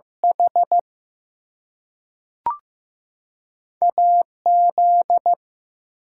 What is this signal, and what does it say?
Morse code, heard as a keyed tone, about 15 wpm, 700 Hz. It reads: H AZ
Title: US States and Canadian Provinces 15wpm